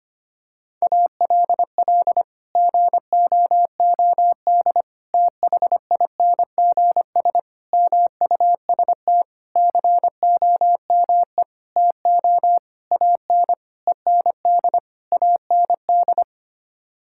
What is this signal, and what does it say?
Morse code, heard as a keyed tone, about 25 wpm, 700 Hz. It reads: ALL GOOD THINGS MUST COME TO AN END AND
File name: Sets of 9 Words - 25wpm